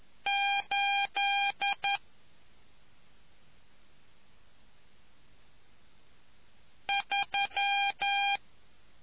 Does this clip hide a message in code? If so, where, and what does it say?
Yes: Morse code from 0.3 to 8.4 s, heard as a keyed tone, about 11 wpm, 800 Hz. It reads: 8 3